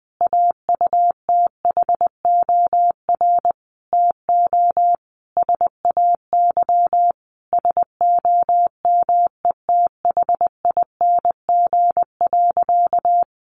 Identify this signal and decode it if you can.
Morse code, heard as a keyed tone, about 20 wpm, 700 Hz. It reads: AUTHOR TO SAY SOMETHING.
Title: Sentences from Top 1200 Words 20wpm